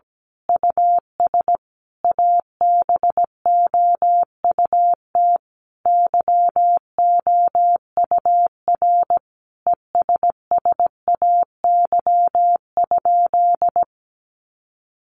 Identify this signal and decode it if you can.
Morse code, heard as a keyed tone, about 17 wpm, 700 Hz. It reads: US ABOUT YOUR ESSAY?
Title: Sentences from Top 1500 Words 17wpm